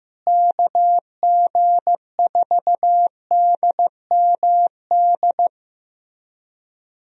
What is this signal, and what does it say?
Morse code, heard as a keyed tone, about 15 wpm, 700 Hz. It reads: KG4DMD
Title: USA Call Signs - 6 Characters - Repeated 3x - 15wpm